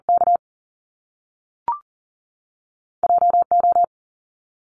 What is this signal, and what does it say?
Morse code, heard as a keyed tone, about 40 wpm, 700 Hz. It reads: K JO
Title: Top Two Letter Combinations in Words 40wpm